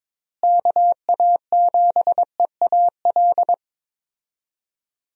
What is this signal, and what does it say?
Morse code, heard as a keyed tone, about 22 wpm, 700 Hz. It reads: KA7EAL
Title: USA Call Signs - 6 Characters - Repeated 3x - 22wpm